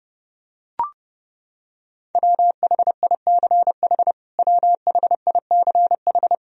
Morse code, heard as a keyed tone, about 30 wpm, 700 Hz. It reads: WHICH WHICH
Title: Top 100 Words Repeated 30wpm